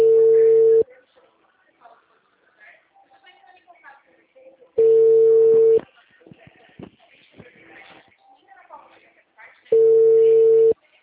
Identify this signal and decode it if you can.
Ringback tone (China, 450 Hz), with speech.
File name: out- -2009- - - .57514